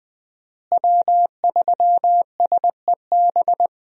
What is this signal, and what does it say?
Morse code, heard as a keyed tone, about 21 wpm, 700 Hz. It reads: W3SEB
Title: USA Call Signs - 5 Characters - Repeated 2x - 20wpm